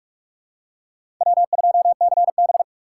Morse code, heard as a keyed tone, about 47 wpm, 700 Hz. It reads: W2CB
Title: USA Call Signs - 4 Characters - 6x Character Spacing - 45wpm